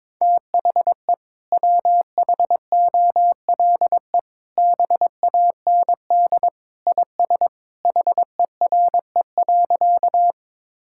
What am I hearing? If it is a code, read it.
Morse code, heard as a keyed tone, about 22 wpm, 700 Hz. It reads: THE WHOLE BAND IS HERE.
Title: Sentences from Top 1600 Words 22wpm